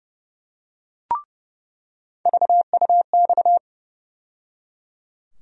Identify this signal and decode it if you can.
Morse code, heard as a keyed tone, about 32 wpm, 700 Hz. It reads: VUX
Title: USA Call Sign Postfixes - 3 Letters 30wpm